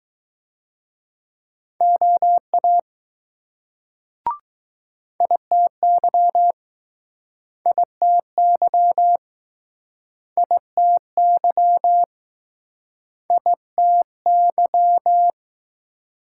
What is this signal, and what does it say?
Morse code, heard as a keyed tone, about 20 wpm, 700 Hz. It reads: OA ITY ITY ITY ITY